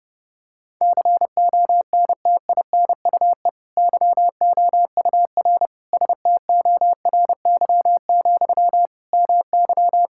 Morse code, heard as a keyed tone, about 30 wpm, 700 Hz. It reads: CONTINUE YOUR STORY, MY
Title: Sentences from Top 1000 Words - Review Part 6 - 30wpm